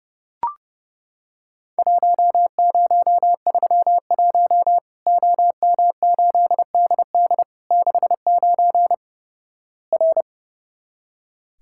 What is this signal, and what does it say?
Morse code, heard as a keyed tone, about 32 wpm, 700 Hz. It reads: 1031 OM8DD 69 R
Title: Worked All Europe DX Contest - QTC Exchange - 30wpm